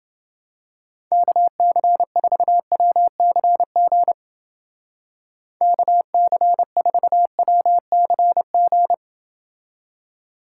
Morse code, heard as a keyed tone, about 29 wpm, 700 Hz. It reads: KC4WCG KC4WCG